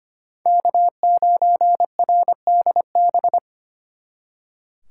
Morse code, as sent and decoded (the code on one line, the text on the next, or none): -.- ----. .-. -.. -...
K9RDB